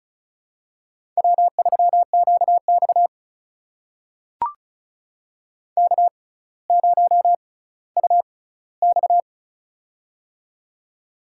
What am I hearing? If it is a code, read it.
Morse code, heard as a keyed tone, about 36 wpm, spaced as for 18 wpm, 700 Hz. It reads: W3QX K0UX
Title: USA Call Signs - 6x Character Spacing - 35wpm